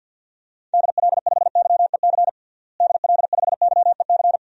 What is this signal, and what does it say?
Morse code, heard as a keyed tone, about 47 wpm, 700 Hz. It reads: DL5YEC DL5YEC
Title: DXCC-100 Call Signs - 6 Characters - Repeated 2x - 50wpm